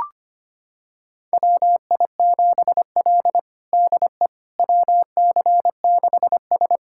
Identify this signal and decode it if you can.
Morse code, heard as a keyed tone, about 25 wpm, 700 Hz. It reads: WI7L DE WC6S
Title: USA Callsign DE Callsign - 4 Characters - 25wpm